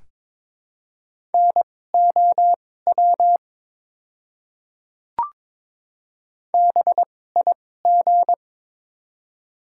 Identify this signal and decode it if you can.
Morse code, heard as a keyed tone, about 22 wpm, spaced as for 16 wpm, 700 Hz. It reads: NOW BIG
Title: 2x Character Spacing - Top 300 Words - 22wpm